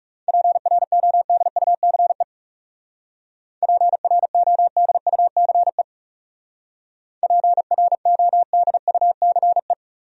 Morse code, heard as a keyed tone, about 36 wpm, 700 Hz. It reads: PRODUCE PRODUCE PRODUCE
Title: Speed-Racing Top 500 Words 30wpm